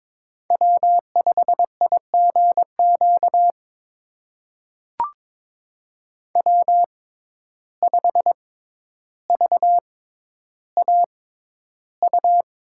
Morse code, heard as a keyed tone, about 22 wpm, spaced as for 11 wpm, 700 Hz. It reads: W5IGQ W5VAU
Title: USA Call Signs - 5 Characters - 6x Character Spacing - 22wpm